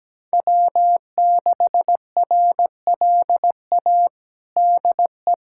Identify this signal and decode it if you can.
Morse code, heard as a keyed tone, about 17 wpm, 700 Hz. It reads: W6RLA DE
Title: USA Callsign DE Callsign - 5 Characters - 17wpm